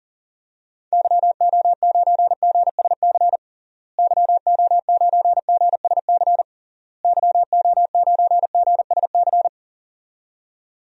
Morse code, heard as a keyed tone, about 38 wpm, 700 Hz. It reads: YO9GSC YO9GSC YO9GSC
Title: DXCC-100 Call Signs - 6 Characters - Repeated 3x - 40wpm